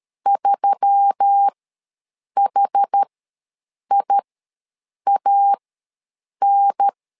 Morse code, heard as a keyed tone, about 13 wpm, spaced as for 7 wpm, 800 Hz. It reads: 3HIAN
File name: LEZIONI / 4 Q1S3 ws 12wpm 2038